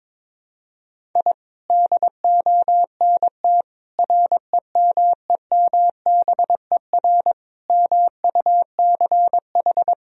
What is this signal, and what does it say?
Morse code, heard as a keyed tone, about 21 wpm, 700 Hz. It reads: I DONT REMEMBER MUCH
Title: Sentences from Top 1000 Words - Review Part 1 - 22wpm